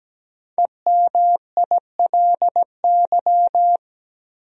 Morse code, heard as a keyed tone, about 17 wpm, 700 Hz. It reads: EMILY
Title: USA Common Names - 17wpm